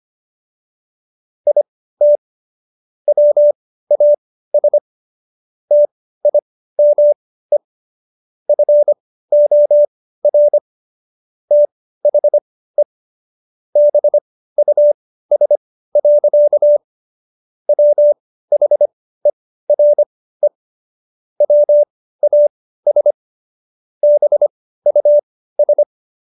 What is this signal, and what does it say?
Morse code, heard as a keyed tone, about 25 wpm, spaced as for 15 wpm, 590 Hz. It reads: IT WAS TIME FOR THE BUS. WHERE WAS BUS